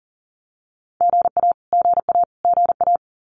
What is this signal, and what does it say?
Morse code, heard as a keyed tone, about 40 wpm, 700 Hz. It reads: GA GA GA
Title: Abbreviations Repeated 3x - 40wpm